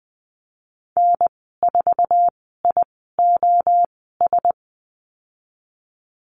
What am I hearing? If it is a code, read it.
Morse code, heard as a keyed tone, about 20 wpm, spaced as for 14 wpm, 700 Hz. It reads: N4IOS
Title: 048 Lesson 9g - Number 4 - Call Signs 20wpm